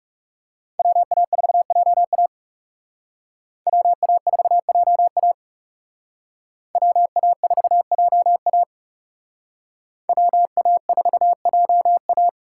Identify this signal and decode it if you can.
Morse code, heard as a keyed tone, about 38 wpm, 700 Hz. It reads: WA4JA WA4JA WA4JA WA4JA